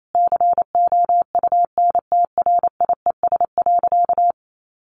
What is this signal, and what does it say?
Morse code, heard as a keyed tone, about 28 wpm, 700 Hz. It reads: COUNTRIES.
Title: A Sound Of Thunder 28wpm